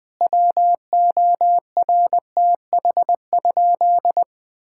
Morse code, heard as a keyed tone, about 20 wpm, 700 Hz. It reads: WORTH?
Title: Sentences from Top 1000 Words - Review Part 6 - 20wpm